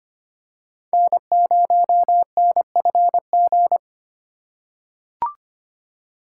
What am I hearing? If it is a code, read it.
Morse code, heard as a keyed tone, about 25 wpm, 700 Hz. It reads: N0NFG
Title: USA Call Signs - 5 Characters - 2x Character Spacing - 25wpm